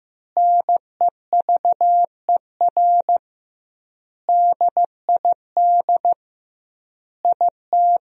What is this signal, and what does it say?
Morse code, heard as a keyed tone, about 16 wpm, 700 Hz. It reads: NEVER DID IT
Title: 2x Word-Spacing - Sentences from Top 300 Words - 15wpm